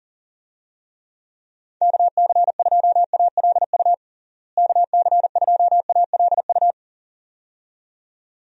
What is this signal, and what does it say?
Morse code, heard as a keyed tone, about 40 wpm, 700 Hz. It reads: KC2ALU KC2ALU
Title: USA Call Signs - 6 Characters - Repeated 2x - 40wpm